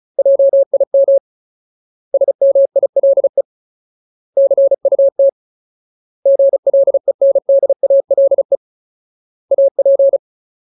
Morse code, heard as a keyed tone, about 34 wpm, 550 Hz. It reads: JIM SMILE CUT GLENDALE AP
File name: Ultimate Word List / 35WPM Ultimate